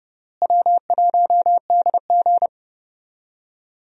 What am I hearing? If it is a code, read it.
Morse code, heard as a keyed tone, about 30 wpm, 700 Hz. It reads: W1DG
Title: USA Call Signs - 4 Characters - Repeated 3x - 30wpm